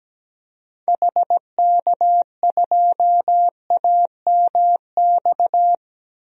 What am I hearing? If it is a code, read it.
Morse code, heard as a keyed tone, about 17 wpm, 700 Hz. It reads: HK2AMX
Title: DXCC-100 Call Signs - 6 Characters - Repeated 2x - 17wpm